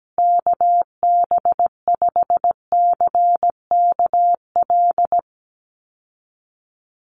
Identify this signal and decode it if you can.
Morse code, heard as a keyed tone, about 17 wpm, 700 Hz. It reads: KB5CKL
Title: USA Call Signs Repeated 3x 17wpm